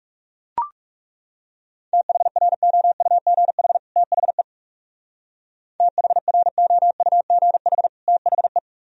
Morse code, heard as a keyed tone, about 37 wpm, 700 Hz. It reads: THROUGH THE THROUGH THE
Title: Speed-Racing - Sets of 2 Words 30wpm